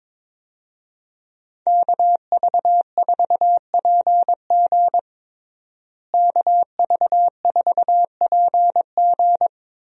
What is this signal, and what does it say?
Morse code, heard as a keyed tone, about 22 wpm, 700 Hz. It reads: KV4PG KV4PG
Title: USA Call Signs - 5 Characters - Repeated 3x - 22wpm